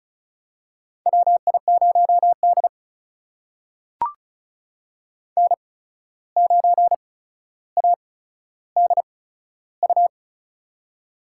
Morse code, heard as a keyed tone, about 36 wpm, spaced as for 14 wpm, 700 Hz. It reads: WI0D N9ADU
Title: USA Call Signs - 8x Character Spacing - 35wpm